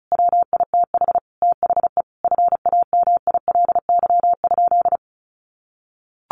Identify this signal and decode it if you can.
Morse code, heard as a keyed tone, about 34 wpm, 700 Hz. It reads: WITH THE FAMILY?
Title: Sentences from Top 1100 Words 35wpm